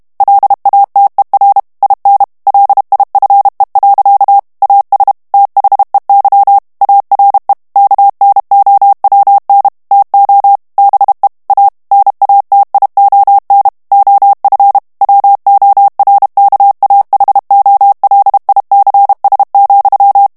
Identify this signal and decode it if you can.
Morse code, heard as a keyed tone, about 32 wpm, 800 Hz. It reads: LATER IN LIFE. AS THEY ARE KNOWN TO BE A NATION OF WORKAHOLICS,
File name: morse mp3 / ANGLAIS texte clair 1 vitesse 30 mots minute